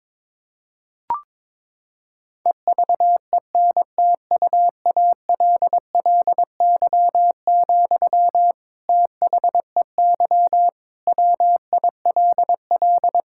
Morse code, heard as a keyed tone, about 22 wpm, 700 Hz. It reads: EVENTUALLY, THEY WILL
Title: Sentences from Top 1000 Words - Review Part 6 - 22wpm